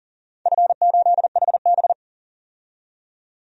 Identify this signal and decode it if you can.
Morse code, heard as a keyed tone, about 40 wpm, 700 Hz. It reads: F8HB
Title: DXCC-100 Call Signs - 4 Characters - 40wpm